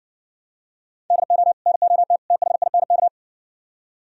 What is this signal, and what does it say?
Morse code, heard as a keyed tone, about 60 wpm, 700 Hz. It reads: DO NOT THINK